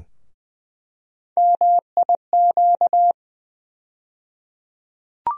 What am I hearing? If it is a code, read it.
Morse code, heard as a keyed tone, about 20 wpm, 700 Hz. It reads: MIQ
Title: USA Call Sign Postfixes - 3 Letters 20wpm